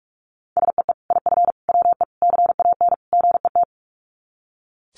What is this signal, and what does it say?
Morse code, heard as a keyed tone, about 44 wpm, 700 Hz. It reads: SEE IF WE CAN GET
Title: Sets of 5 Words Encore2 45wpm